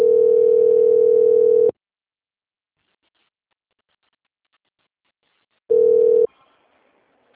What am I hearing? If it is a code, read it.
Ringback tone (North America), 440 + 480 Hz.
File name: out- - - - - .22883